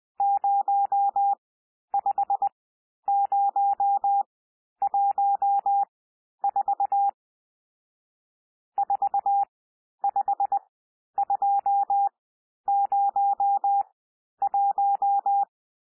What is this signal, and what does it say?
Morse code, heard as a keyed tone, about 20 wpm, spaced as for 11 wpm, 800 Hz. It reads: 05014 45201
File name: cwmp3 / N1 SOUND NUMBERS 20 wpm